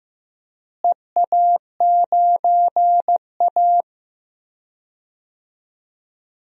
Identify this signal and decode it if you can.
Morse code, heard as a keyed tone, about 15 wpm, 700 Hz. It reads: EA9A
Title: DXCC-100 Call Signs - 4 Characters - Repeated 2x - 15wpm